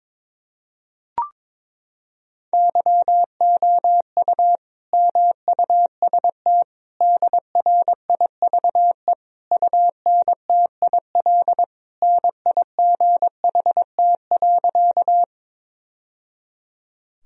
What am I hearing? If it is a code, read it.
Morse code, heard as a keyed tone, about 22 wpm, 700 Hz. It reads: YOU MUST DRIVE UNTIL NIGHT.
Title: Sentences from Top 1000 Words - Review Part 2 - 22wpm